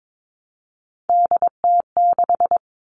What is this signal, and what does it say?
Morse code, heard as a keyed tone, about 22 wpm, 700 Hz. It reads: DT6
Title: All Cut Numbers - 3-Digit 22wpm